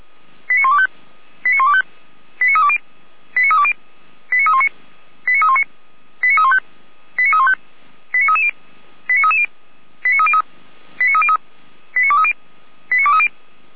Five-tone selective call (ZVEI), heard as quick runs of five tones, tones of about 70 ms. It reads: 89125 89125 89320 89320 89319 89319 89315 89315 89330 89330 89303 89303 89230 89230